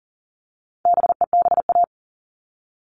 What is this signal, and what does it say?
Morse code, heard as a keyed tone, about 40 wpm, 700 Hz. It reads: BEBA
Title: All Cut Numbers - 4-Digit 40wpm